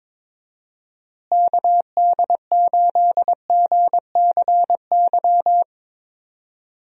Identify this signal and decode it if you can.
Morse code, heard as a keyed tone, about 22 wpm, 700 Hz. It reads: KD8GCY
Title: USA Call Signs - 6 Characters - Repeated 3x - 22wpm